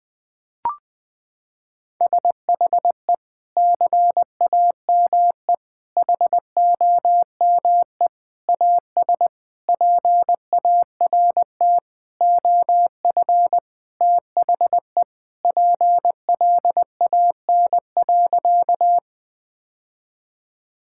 Morse code, heard as a keyed tone, about 20 wpm, 700 Hz. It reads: SHE CAME HOME AS PART OF THE PLAN.
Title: Sentences with 4-Word Phrases - 20wpm